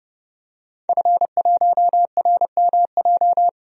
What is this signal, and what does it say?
Morse code, heard as a keyed tone, about 30 wpm, 700 Hz. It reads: F1RMJ
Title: DXCC-100 Call Signs - 5 Characters - 30wpm